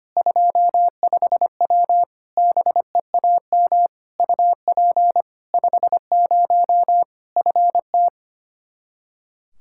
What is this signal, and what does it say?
Morse code, heard as a keyed tone, about 25 wpm, 700 Hz. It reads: 25W BEAM UP 50 FT
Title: K1USN SST QSO Practice 25wpm